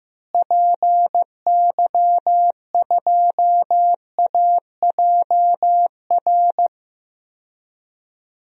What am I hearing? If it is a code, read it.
Morse code, heard as a keyed tone, about 15 wpm, 700 Hz. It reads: PY2AJR
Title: DXCC-100 Call Signs - 6 Characters - Repeated 3x - 15wpm